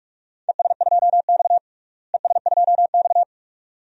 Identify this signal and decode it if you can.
Morse code, heard as a keyed tone, about 42 wpm, 700 Hz. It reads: ES2X ES2X
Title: DXCC-100 Call Signs - 4 Characters - Repeated 2x - 45wpm